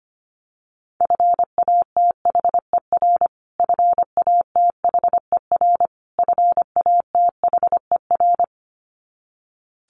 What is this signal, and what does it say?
Morse code, heard as a keyed tone, about 25 wpm, 700 Hz. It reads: FATHER FATHER FATHER
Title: Top 500 Words Repeated 25wpm